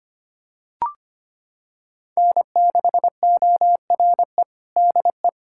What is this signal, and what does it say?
Morse code, heard as a keyed tone, about 25 wpm, 700 Hz. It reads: N6ORE DE
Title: USA Callsign DE Callsign - 5 Characters - 25wpm